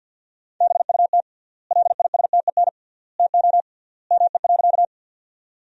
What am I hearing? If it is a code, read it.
Morse code, heard as a keyed tone, about 50 wpm, 700 Hz. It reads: BUT LISTEN TO ME.
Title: 3x Word-Spacing - Sentences from Top 400 Words - 50wpm